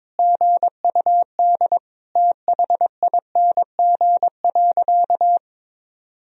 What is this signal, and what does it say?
Morse code, heard as a keyed tone, about 21 wpm, 700 Hz. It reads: GUD THING.
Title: Sentences from Top 300 Words and Abbreviations 22wpm